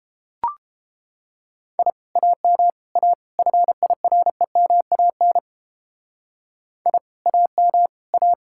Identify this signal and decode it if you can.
Morse code, heard as a keyed tone, about 32 wpm, 700 Hz. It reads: I AM A FIREMAN I AM A